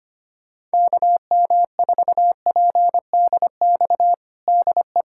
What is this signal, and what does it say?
Morse code, heard as a keyed tone, about 25 wpm, 700 Hz. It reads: KM4PDX DE
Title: USA Callsign DE Callsign - 6 Characters - 25wpm